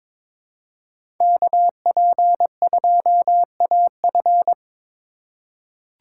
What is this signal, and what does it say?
Morse code, heard as a keyed tone, about 22 wpm, 700 Hz. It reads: KP2AF
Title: USA Call Signs - 5 Characters - 3x Character Spacing - 22wpm